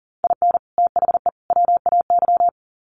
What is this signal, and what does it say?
Morse code, heard as a keyed tone, about 40 wpm, 700 Hz. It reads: IN THE WAY
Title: Sets of 3 Words 40wpm